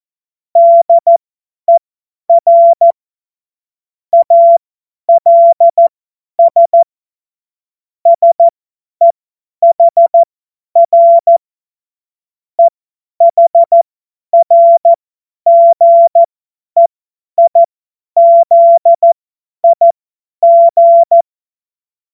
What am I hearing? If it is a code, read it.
Morse code, heard as a keyed tone, about 14 wpm, spaced as for 10 wpm, 680 Hz. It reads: DER ALS SEHR EHRGEIZIG
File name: Morsekurs / Witze / Witz 04. Januar 2022, 50 F70